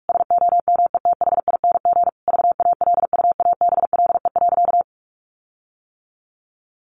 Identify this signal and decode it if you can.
Morse code, heard as a keyed tone, about 44 wpm, 700 Hz. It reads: SOMETHING VALUABLE.
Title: Sentences from Top 1400 Words 45wpm